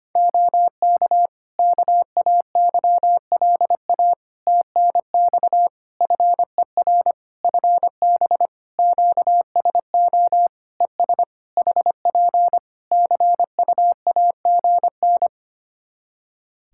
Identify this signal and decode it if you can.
Morse code, heard as a keyed tone, about 25 wpm, 700 Hz. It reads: OK KAYLA TNX FER FB QSO ES HP CUAGN
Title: Practice QSO - Exchange 4 - Ending - 25wpm